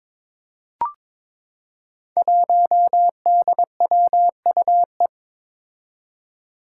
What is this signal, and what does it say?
Morse code, heard as a keyed tone, about 22 wpm, 700 Hz. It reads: 1DWUE
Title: Five Challenging Letter-Number Combinations 22wpm